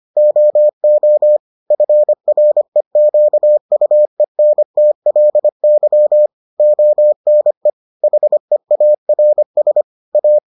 Morse code, heard as a keyed tone, about 25 wpm, 600 Hz. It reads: OO FREQUENTLY ONE HEARS A